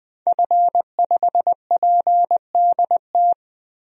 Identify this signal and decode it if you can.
Morse code, heard as a keyed tone, about 20 wpm, 700 Hz. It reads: F5PDT
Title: DXCC-100 Call Signs - 5 Characters - Repeated 2x - 20wpm